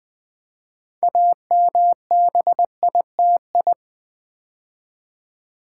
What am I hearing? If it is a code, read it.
Morse code, heard as a keyed tone, about 20 wpm, 700 Hz. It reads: AMBITI
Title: Build A Word - 3 Letters At A Time - 8-Letter Words - 20wpm